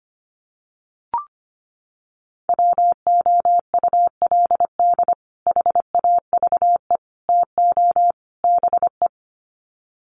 Morse code, heard as a keyed tone, about 25 wpm, 700 Hz. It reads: WOULD HAVE TO BE
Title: Sets of 4 Words 25wpm